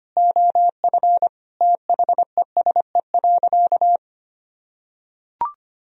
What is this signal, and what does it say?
Morse code, heard as a keyed tone, about 25 wpm, 700 Hz. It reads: OF THESE.
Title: Sentences from Top 1000 Words - Review Part 1 - 25wpm